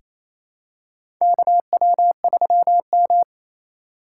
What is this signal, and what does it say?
Morse code, heard as a keyed tone, about 28 wpm, 700 Hz. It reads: KW3M
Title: USA Call Signs - 3x Character Spacing - 28wpm